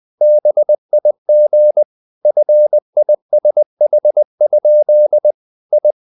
Morse code, heard as a keyed tone, about 20 wpm, 600 Hz. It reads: BIG FISH? I
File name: the old man and the sea / the old man and the sea part 1 20wpm